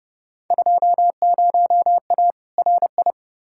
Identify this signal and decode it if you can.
Morse code, heard as a keyed tone, about 31 wpm, 700 Hz. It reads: 20A RI
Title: Field Day Exchanges Encore - 30wpm